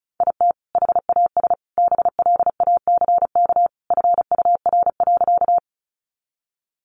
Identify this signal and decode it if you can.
Morse code, heard as a keyed tone, about 34 wpm, 700 Hz. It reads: IT HAS BLACK FUR.
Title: Easy Phrases 35wpm